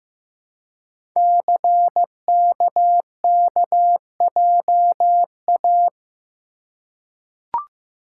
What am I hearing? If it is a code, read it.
Morse code, heard as a keyed tone, about 15 wpm, 700 Hz. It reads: CKKJA
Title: Code Groups - 5 Letters - 15wpm